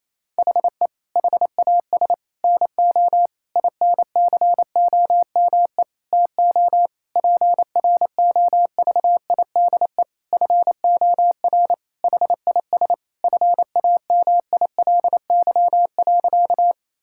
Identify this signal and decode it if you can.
Morse code, heard as a keyed tone, about 28 wpm, 700 Hz. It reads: HE HAS NO INCOME TO PROVIDE FOR HIS FAMILY.
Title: Sentences from Top 1000 Words - Review Part 5 - 28wpm